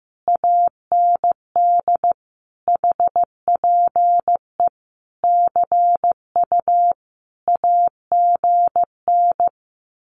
Morse code, heard as a keyed tone, about 15 wpm, 700 Hz. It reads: AND HPE CU AGN
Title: Common QSO Phrases Encore 15wpm